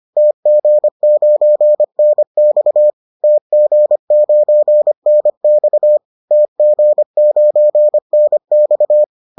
Morse code, heard as a keyed tone, about 25 wpm, 600 Hz. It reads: TG9NX TG9NX TG9NX
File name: 25wpm-6